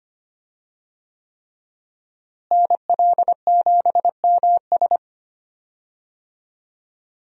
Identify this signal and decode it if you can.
Morse code, heard as a keyed tone, about 25 wpm, 700 Hz. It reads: NL7MS
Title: USA Call Signs - 5 Characters - Repeated 3x - 25wpm